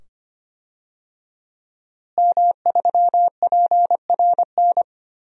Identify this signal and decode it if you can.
Morse code, heard as a keyed tone, about 25 wpm, 700 Hz. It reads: M3PRN
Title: DXCC-100 Call Signs - 5 Characters - Repeated 2x - 25wpm